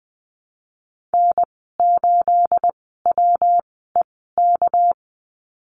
Morse code, heard as a keyed tone, about 20 wpm, spaced as for 15 wpm, 700 Hz. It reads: N8WEK
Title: 238 Lesson 38g - Letter q - Call Signs 20wpm